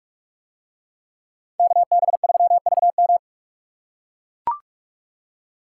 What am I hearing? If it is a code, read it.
Morse code, heard as a keyed tone, about 40 wpm, 700 Hz. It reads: KB3VM